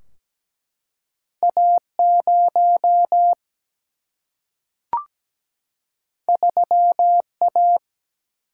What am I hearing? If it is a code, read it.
Morse code, heard as a keyed tone, about 17 wpm, 700 Hz. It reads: A0 3A
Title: Single-Letter Single-Number 17wpm